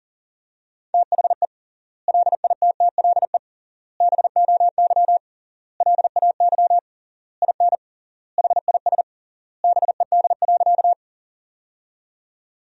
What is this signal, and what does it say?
Morse code, heard as a keyed tone, about 40 wpm, 700 Hz. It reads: THE LITTLE BOY LAY IN HIS BED.
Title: 3x Word-Spacing - Sentences from Top 500 Words - 40wpm